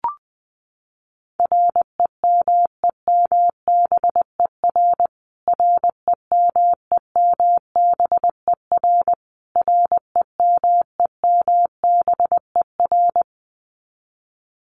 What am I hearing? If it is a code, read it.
Morse code, heard as a keyed tone, about 20 wpm, 700 Hz. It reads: REMEMBER REMEMBER REMEMBER
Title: Top 500 Words Repeated 20wpm